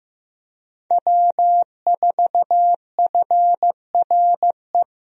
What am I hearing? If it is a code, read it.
Morse code, heard as a keyed tone, about 15 wpm, 700 Hz. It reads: W4FRE
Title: USA Call Signs - 5 Characters - Repeated 3x - 15wpm